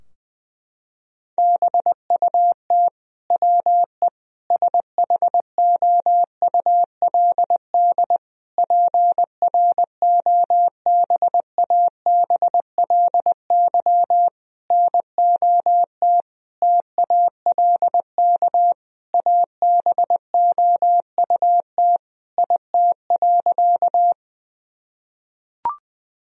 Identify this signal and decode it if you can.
Morse code, heard as a keyed tone, about 20 wpm, 700 Hz. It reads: BUT WE SHOULD PROBABLY NOT TALK ABOUT IT.
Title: Sentences from Top 1000 Words - Review Part 4 - 20wpm